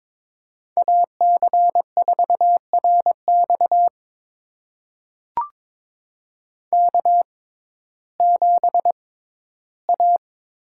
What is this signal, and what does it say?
Morse code, heard as a keyed tone, about 22 wpm, spaced as for 14 wpm, 700 Hz. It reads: AC4RX K7A
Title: USA Call Signs - 5 Characters - 6x Character Spacing - 22wpm